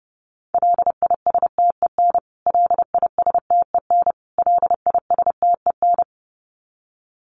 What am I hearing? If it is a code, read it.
Morse code, heard as a keyed tone, about 30 wpm, 700 Hz. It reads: LISTEN LISTEN LISTEN
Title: Top 500 Words Repeated 30wpm